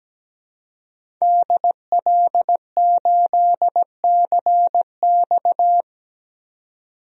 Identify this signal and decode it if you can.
Morse code, heard as a keyed tone, about 17 wpm, 700 Hz. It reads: DL8CX
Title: DXCC-100 Call Signs - 5 Characters - 17wpm